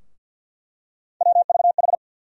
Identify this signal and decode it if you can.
Morse code, heard as a keyed tone, about 53 wpm, 700 Hz. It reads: WVH
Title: USA Call Sign Postfixes - 3 Letters 50wpm